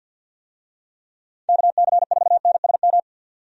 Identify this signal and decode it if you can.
Morse code, heard as a keyed tone, about 50 wpm, 700 Hz. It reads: KC4NSM
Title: USA Call Signs - 6 Characters - 50wpm